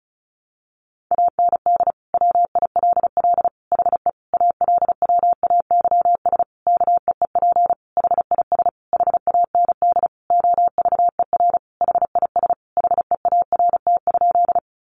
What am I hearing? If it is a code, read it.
Morse code, heard as a keyed tone, about 34 wpm, 700 Hz. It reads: AND WILL HE ALWAYS KEEP HIS HAND OVER HIS HEART?
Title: Sentences from Top 1200 Words 35wpm